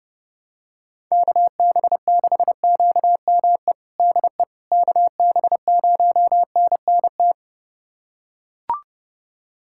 Morse code, heard as a keyed tone, about 29 wpm, 700 Hz. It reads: KB6QME DE KB0NNT
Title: USA Callsign DE Callsign - 6 Characters - 30wpm